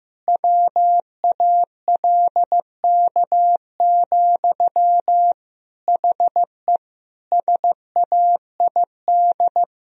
Morse code, heard as a keyed tone, about 16 wpm, 700 Hz. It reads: WALK, HE SAID
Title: Sentences from Top 1000 Words - Review Part 2 - 15wpm